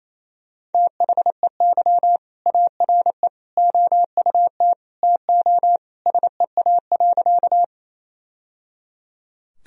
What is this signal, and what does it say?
Morse code, heard as a keyed tone, about 28 wpm, 700 Hz. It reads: THEY ARE OUT TO SEA.
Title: Sentences from Top 1000 Words - Review Part 1 - 28wpm